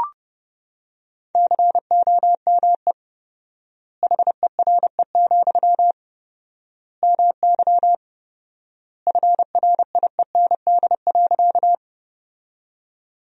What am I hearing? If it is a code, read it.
Morse code, heard as a keyed tone, about 32 wpm, 700 Hz. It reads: COME HERE, MY FRIEND.
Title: 4x Word-Spacing - Sentences from Top 300 Words - 30wpm